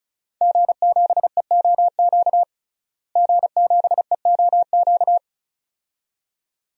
Morse code, heard as a keyed tone, about 36 wpm, 700 Hz. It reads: G7EOQ G7EOQ
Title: DXCC-100 Call Signs - 5 Characters - Repeated 2x - 35wpm